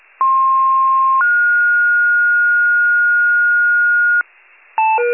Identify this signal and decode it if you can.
Two-tone page, heard as a one-second tone, then a three-second tone, about 1060 then 1430 Hz, with noise. It is heard from 0.2 to 4.2 s.